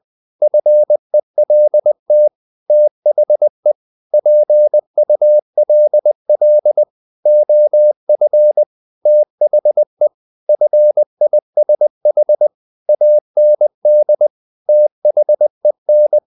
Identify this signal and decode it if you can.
Morse code, heard as a keyed tone, about 20 wpm, 600 Hz. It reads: FELT THE PULL OF THE FISH AND THEN
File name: the old man and the sea / the old man and the sea part 1 20wpm